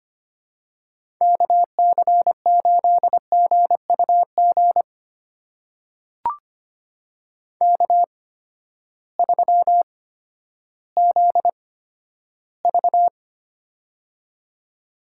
Morse code, heard as a keyed tone, about 25 wpm, spaced as for 12 wpm, 700 Hz. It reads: KC8GUG K3ZV
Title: USA Call Signs - 8x Character Spacing - 25wpm